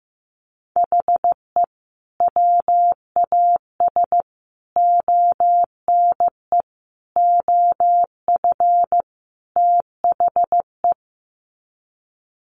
Morse code, heard as a keyed tone, about 15 wpm, 700 Hz. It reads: HE WAS ONE OF THE
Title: Sets of 5 Words 15wpm